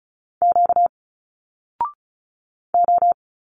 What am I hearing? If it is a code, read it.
Morse code, heard as a keyed tone, about 36 wpm, 700 Hz. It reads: Q O
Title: Single Letters 35wpm